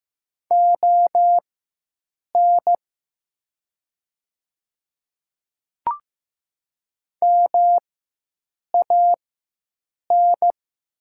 Morse code, heard as a keyed tone, about 15 wpm, spaced as for 7 wpm, 700 Hz. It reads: ON MAN
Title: 4x Character Spacing - Top 200 Words - 15wpm